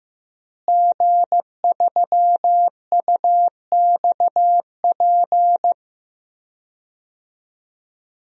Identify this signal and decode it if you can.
Morse code, heard as a keyed tone, about 15 wpm, 700 Hz. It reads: G3UXP
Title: DXCC-100 Call Signs - 5 Characters - Repeated 2x - 15wpm